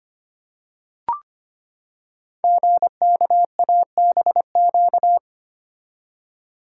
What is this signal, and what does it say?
Morse code, heard as a keyed tone, about 25 wpm, 700 Hz. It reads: GKABQ
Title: Code Groups - 5 Letters - 25wpm